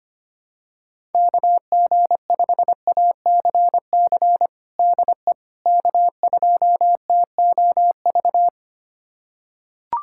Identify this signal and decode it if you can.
Morse code, heard as a keyed tone, about 25 wpm, 700 Hz. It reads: KG5ACC DE K2TOV
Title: USA Callsign DE Callsign 25wpm